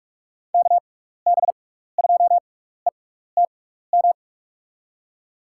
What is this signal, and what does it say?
Morse code, heard as a keyed tone, about 47 wpm, spaced as for 16 wpm, 700 Hz. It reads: KB2ETM
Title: USA Call Signs - 6 Characters - 6x Character Spacing - 45wpm